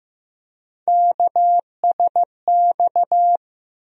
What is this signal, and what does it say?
Morse code, heard as a keyed tone, about 15 wpm, 700 Hz. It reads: KSX
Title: USA Call Sign Postfixes - 3 Letters 15wpm